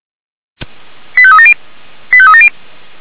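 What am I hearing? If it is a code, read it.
Five-tone selective call (ZVEI), heard as quick runs of five tones, tones of about 70 ms. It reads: 85370 85370
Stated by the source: Five-tone ZVEI selective call received on SDR.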